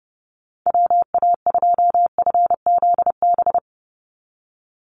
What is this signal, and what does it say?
Morse code, heard as a keyed tone, about 30 wpm, 700 Hz. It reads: WA2FZB
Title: Call Signs - 30wpm